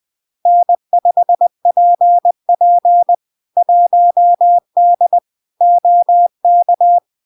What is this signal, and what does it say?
Morse code, heard as a keyed tone, about 20 wpm, 700 Hz. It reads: N5PP 1D OK
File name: LARK-FD-2022-Log-in-CW-20wpm-700-Hz